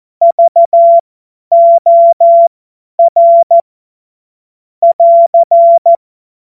Morse code, heard as a keyed tone, about 14 wpm, spaced as for 10 wpm, 680 Hz. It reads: VOR <AR>
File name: Morsekurs / Witze / Witz 22. Juni 2021, 50 F70